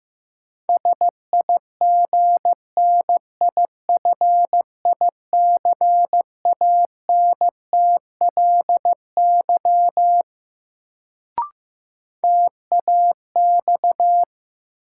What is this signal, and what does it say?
Morse code, heard as a keyed tone, about 15 wpm, 700 Hz. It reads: SIGNIFICANTLY TAX
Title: Sentences from Top 1500 Words - New Word Drill Alt - 15wpm